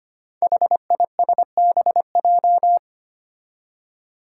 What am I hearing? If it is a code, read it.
Morse code, heard as a keyed tone, about 25 wpm, 700 Hz. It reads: HISBJ
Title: Five Challenging Letter-Number Combinations 25wpm